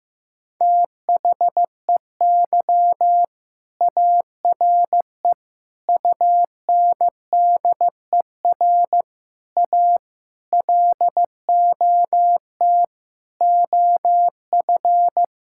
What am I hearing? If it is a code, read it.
Morse code, heard as a keyed tone, about 15 wpm, 700 Hz. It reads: THEY ARE UNDER A LOT OF
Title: Sentences from Top 1000 Words - Review Part 6 - 15wpm